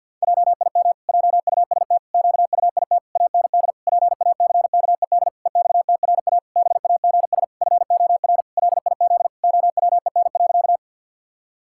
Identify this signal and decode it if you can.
Morse code, heard as a keyed tone, about 50 wpm, 700 Hz. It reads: JIM JUST QUIT AND PACKED EXTRA BAGS FOR LIZ OWEN.
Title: Pangram Challenge v2 - 50wpm